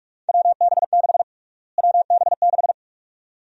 Morse code, heard as a keyed tone, about 46 wpm, 700 Hz. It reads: WB6 WB6
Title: USA Call Sign Prefixes - Repeated 2x - 45wpm